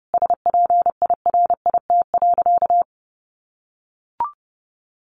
Morse code, heard as a keyed tone, about 30 wpm, 700 Hz. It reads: SPIRIT.
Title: Sentences from Top 1400 Words 30wpm